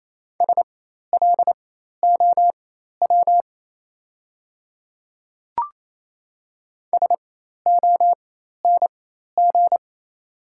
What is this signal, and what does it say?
Morse code, heard as a keyed tone, about 28 wpm, spaced as for 13 wpm, 690 Hz. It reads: SLOW SONG
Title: 4x Character Spacing - Top 500 Words - 28wpm